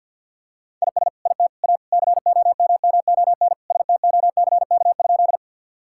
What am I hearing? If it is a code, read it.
Morse code, heard as a keyed tone, about 50 wpm, 700 Hz. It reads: IS IT A COMMON STOCK?
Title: Sentences from Top 1000 Words - Review Part 5 - 50wpm